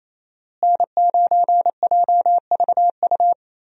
Morse code, heard as a keyed tone, about 28 wpm, 700 Hz. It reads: N9JVU
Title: USA Call Signs - 5 Characters - Repeated 2x - 28wpm